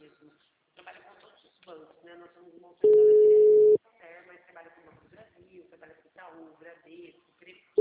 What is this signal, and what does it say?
Ringback tone (Europe, 425 Hz), with speech.
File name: out- -2018- - - .82009